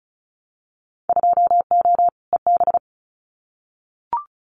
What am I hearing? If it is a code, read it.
Morse code, heard as a keyed tone, about 36 wpm, 700 Hz. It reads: 2O EB
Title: Winter Field Day Exchanges Encore 35wpm